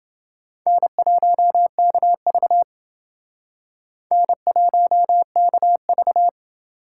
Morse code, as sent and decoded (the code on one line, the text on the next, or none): -. .---- -.- ...- / -. .---- -.- ...-
N1KV N1KV